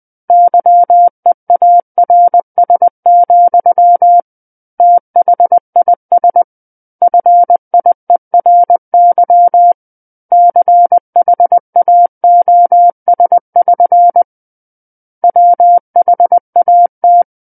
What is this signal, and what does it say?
Morse code, heard as a keyed tone, about 20 wpm, 700 Hz. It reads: YEARS, THIS FIERY CHAOS<SN> WHAT
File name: 20 WPM